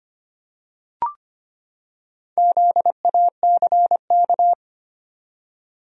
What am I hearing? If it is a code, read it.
Morse code, heard as a keyed tone, about 25 wpm, 700 Hz. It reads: ZACK
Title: USA Common Names - Max 4 Letters - 25wpm